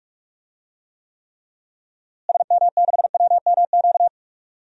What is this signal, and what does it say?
Morse code, heard as a keyed tone, about 45 wpm, 700 Hz. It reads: SM6WMQ
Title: DXCC-100 Call Signs - 6 Characters - Repeated 3x - 45wpm